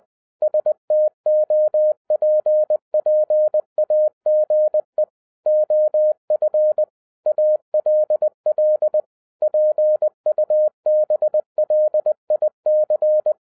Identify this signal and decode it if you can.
Morse code, heard as a keyed tone, about 20 wpm, 610 Hz. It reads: STOPPAGE OF ALL PUBLIC
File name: 2011-07-mp3 / n33 THE 3.11 2 20wpm